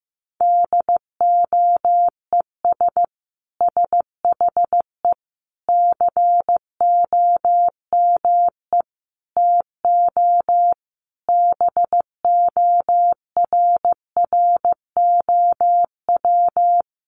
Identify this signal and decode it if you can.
Morse code, heard as a keyed tone, about 15 wpm, 700 Hz. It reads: DOES SHE COME TO BORROW
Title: Sentences from Top 1500 Words 15wpm